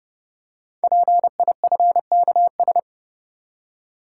Morse code, heard as a keyed tone, about 30 wpm, 700 Hz. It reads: PIFKS
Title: Code Groups - 5 Letters - 30wpm